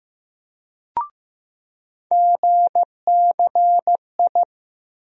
Morse code, heard as a keyed tone, about 15 wpm, 700 Hz. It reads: GCI
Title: USA Call Sign Postfixes - 3 Letters 15wpm